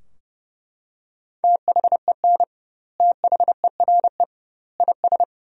Morse code, heard as a keyed tone, about 30 wpm, 700 Hz. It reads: THEN THERE IS
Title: 2x Word-Spacing - Sentences from Top 100 Words - 30wpm